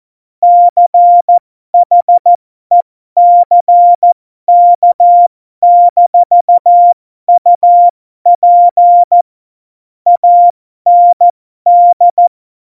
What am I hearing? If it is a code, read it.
Morse code, heard as a keyed tone, about 14 wpm, spaced as for 12 wpm, 700 Hz. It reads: CHECK-UP AND
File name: Morsekurs / Witze